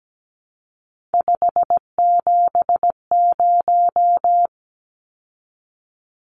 Morse code, heard as a keyed tone, about 17 wpm, 700 Hz. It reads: 570
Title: Numbers - 3-Digit 17wpm